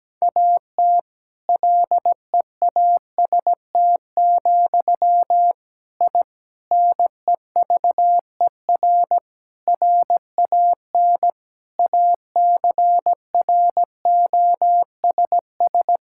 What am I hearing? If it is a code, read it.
Morse code, heard as a keyed tone, about 17 wpm, 700 Hz. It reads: AT LEAST, I NEVER RAN ACROSS
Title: Sentences from Top 1000 Words - Review Part 5 - 17wpm